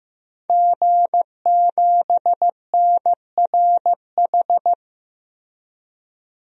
Morse code, heard as a keyed tone, about 15 wpm, 700 Hz. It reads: G7NRH
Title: DXCC-100 Call Signs - 5 Characters - Repeated 3x - 15wpm